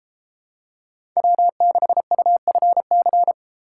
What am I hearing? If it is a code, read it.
Morse code, heard as a keyed tone, about 31 wpm, 700 Hz. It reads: W6UFC